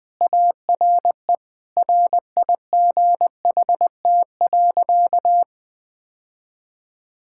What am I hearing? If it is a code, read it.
Morse code, heard as a keyed tone, about 20 wpm, 700 Hz. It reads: ARE RIGHT.
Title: Sentences with 3-Word Phrases - 20wpm